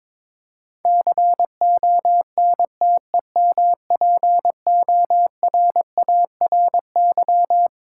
Morse code, heard as a keyed tone, about 22 wpm, 690 Hz. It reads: CONTEMPORARY
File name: Difficult Words with Hints - 22wpm